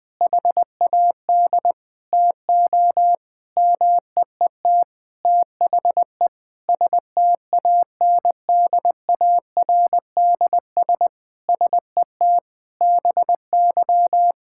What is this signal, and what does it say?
Morse code, heard as a keyed tone, about 20 wpm, 700 Hz. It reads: HAD TO MEET THE STANDARDS SET BY
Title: Sentences from Top 1000 Words - Review Part 6 - 20wpm